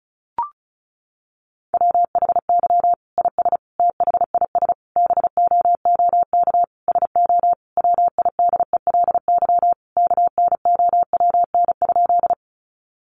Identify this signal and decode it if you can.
Morse code, heard as a keyed tone, about 34 wpm, 700 Hz. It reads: WHY IS THIS BOOK SO WIDELY KNOWN?
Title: Sentences from Top 1200 Words 35wpm